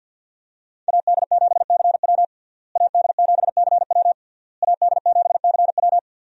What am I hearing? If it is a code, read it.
Morse code, heard as a keyed tone, about 52 wpm, 700 Hz. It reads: AD7CW AD7CW AD7CW
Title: USA Call Signs - 5 Characters - Repeated 3x - 50wpm